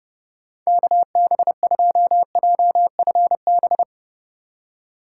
Morse code, heard as a keyed tone, about 27 wpm, 700 Hz. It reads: KB2JFB